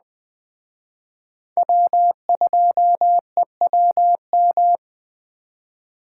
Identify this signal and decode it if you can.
Morse code, heard as a keyed tone, about 20 wpm, 700 Hz. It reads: W2EWM
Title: Speed Racing - USA 1x3 Call Signs 17wpm